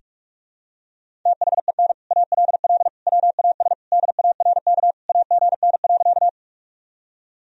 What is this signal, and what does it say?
Morse code, heard as a keyed tone, about 46 wpm, 700 Hz. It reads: THEN ALL WAS DARK AGN.
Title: Sentences from Top 500 Words and Abbreviations 45wpm